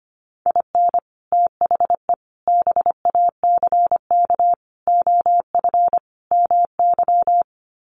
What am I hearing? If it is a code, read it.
Morse code, heard as a keyed tone, about 25 wpm, 700 Hz. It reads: IN THE BACK OF MY
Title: Sets of 5 Words Encore 25wpm